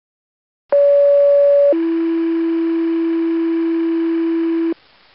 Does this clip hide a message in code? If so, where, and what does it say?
Yes: two-tone page from 0.7 to 4.7 s, heard as a one-second tone, then a three-second tone, about 570 then 330 Hz, with noise.